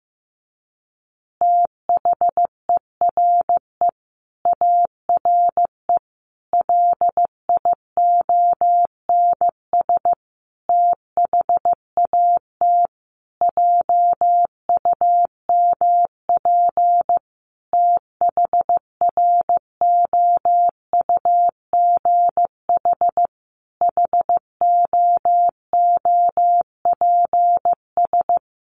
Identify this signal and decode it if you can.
Morse code, heard as a keyed tone, about 15 wpm, 700 Hz. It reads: THERE ARE LIONS THAT JUMP THROUGH HOOPS
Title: Easy Phrases 15wpm